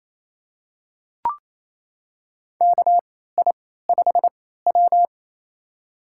Morse code, heard as a keyed tone, about 28 wpm, spaced as for 16 wpm, 700 Hz. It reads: KI5W
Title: USA Call Signs - 3x Character Spacing - 28wpm